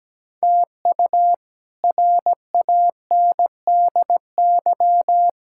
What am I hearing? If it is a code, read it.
Morse code, heard as a keyed tone, about 17 wpm, 700 Hz. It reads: TU RANDY
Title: K1USN SST QSO Practice - 17wpm